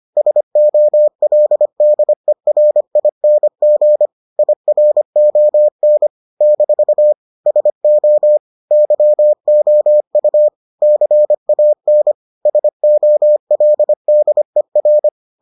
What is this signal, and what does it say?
Morse code, heard as a keyed tone, about 25 wpm, 600 Hz. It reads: SOLDERING IRON - SO YOU CAN SOLDER